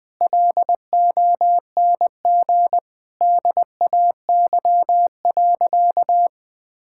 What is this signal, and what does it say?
Morse code, heard as a keyed tone, about 20 wpm, 700 Hz. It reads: LONG DAY.
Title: Sentences with 4-Word Phrases - Spoken Beginning - 20wpm